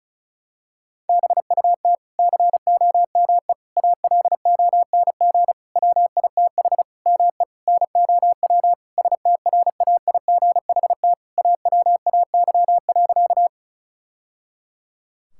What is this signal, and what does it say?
Morse code, heard as a keyed tone, about 36 wpm, 700 Hz. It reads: BUT COME ALONG WITH ME NOW STRAIGHT AWAY.
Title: Sentences from Top 1000 Words - Review Part 6 - 35wpm